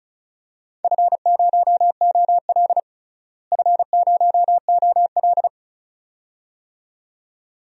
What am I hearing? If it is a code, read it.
Morse code, heard as a keyed tone, about 36 wpm, 700 Hz. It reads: F0OL F0OL
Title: DXCC-100 Call Signs - 4 Characters - Repeated 2x - 35wpm